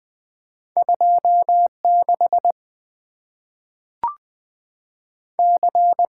Morse code, heard as a keyed tone, about 20 wpm, 690 Hz. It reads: 2 6 C
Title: Two Letter-Number Combinations - 20wpm